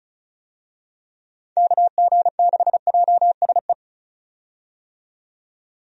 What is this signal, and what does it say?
Morse code, heard as a keyed tone, about 36 wpm, 700 Hz. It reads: KG6JSE
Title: USA Call Signs - 6 Characters - 35wpm